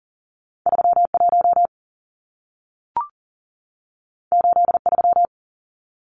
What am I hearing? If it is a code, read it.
Morse code, heard as a keyed tone, about 40 wpm, 700 Hz. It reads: 31 83
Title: Numbers - 2-Digit 40wpm